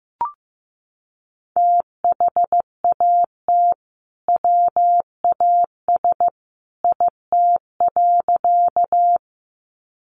Morse code, heard as a keyed tone, about 15 wpm, 700 Hz. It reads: THAT WAS IT.
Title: Sentences from Top 100 Words 15wpm